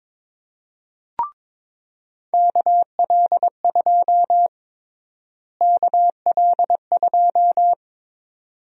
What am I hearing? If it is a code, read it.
Morse code, heard as a keyed tone, about 21 wpm, 700 Hz. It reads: KL2 KL2
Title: USA Call Sign Prefixes - Repeated 2x - 22wpm